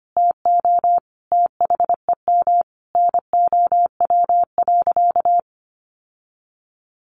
Morse code, heard as a keyed tone, about 25 wpm, 700 Hz. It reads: TO THEM NOW.
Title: Sentences from Top 1300 Words 25wpm